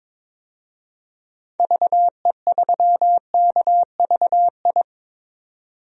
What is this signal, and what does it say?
Morse code, heard as a keyed tone, about 22 wpm, 700 Hz. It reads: VE3KVI
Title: DXCC-100 Call Signs - 6 Characters - 22wpm